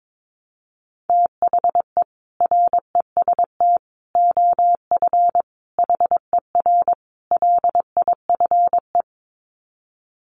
Morse code, heard as a keyed tone, about 22 wpm, 700 Hz. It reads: THE REST OF HER LIFE
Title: Sets of 5 Words Encore 22wpm